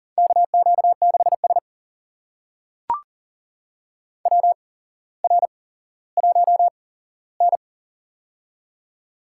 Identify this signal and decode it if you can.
Morse code, heard as a keyed tone, about 40 wpm, spaced as for 17 wpm, 700 Hz. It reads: KQ6S WR1N
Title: USA Call Signs - 4 Characters - 8x Character Spacing - 40wpm